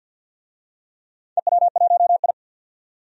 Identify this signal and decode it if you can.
Morse code, heard as a keyed tone, about 51 wpm, 700 Hz. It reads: EW1I
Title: DXCC-100 Call Signs - 4 Characters - 50wpm